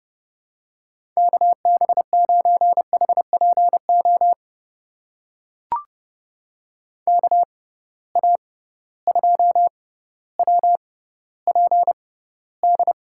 Morse code, heard as a keyed tone, about 32 wpm, spaced as for 16 wpm, 700 Hz. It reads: KB9HPO KA2WPD
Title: USA Call Signs - 6x Character Spacing - 30wpm